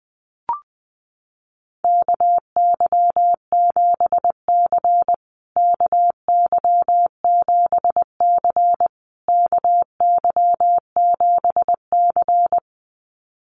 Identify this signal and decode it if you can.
Morse code, heard as a keyed tone, about 20 wpm, 700 Hz. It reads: KY7C KY7C KY7C
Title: USA Call Signs Repeated 3x 20wpm